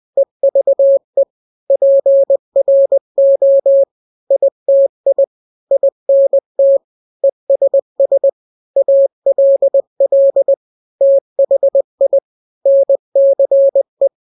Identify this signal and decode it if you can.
Morse code, heard as a keyed tone, about 20 wpm, 550 Hz. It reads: EVE PRO ITI INT ESS ALL THI NCE
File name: n-grams / 3-grams 20 18 WPM